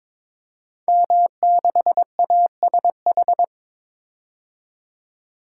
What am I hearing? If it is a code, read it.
Morse code, heard as a keyed tone, about 22 wpm, 700 Hz. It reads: M6ASH
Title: Code Groups - 5 Letter:Numbers - 22wpm